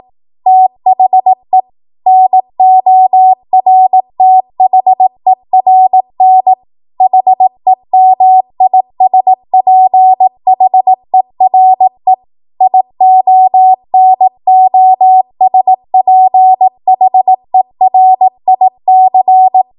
Morse code, heard as a keyed tone, about 18 wpm, 750 Hz. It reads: THE NORTHERN HEMISPHERE IONOSPHERIC
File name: Morse / 18WPM